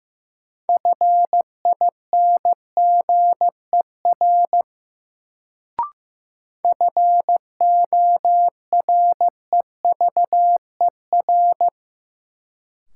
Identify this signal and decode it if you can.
Morse code, heard as a keyed tone, about 15 wpm, 700 Hz. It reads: FINGER FOREVER
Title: Sentences from Top 1600 Words - New Word Drill - 15wpm